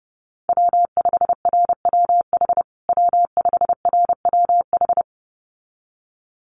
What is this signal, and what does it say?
Morse code, heard as a keyed tone, about 30 wpm, 700 Hz. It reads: W5RWH W5RWH
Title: USA Call Signs Repeated 2x 30wpm